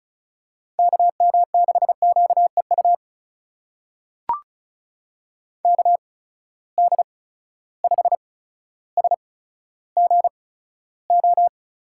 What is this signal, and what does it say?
Morse code, heard as a keyed tone, about 36 wpm, spaced as for 15 wpm, 700 Hz. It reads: KM6QEU KD5SGO
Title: USA Call Signs - 6 Characters - 8x Character Spacing - 35wpm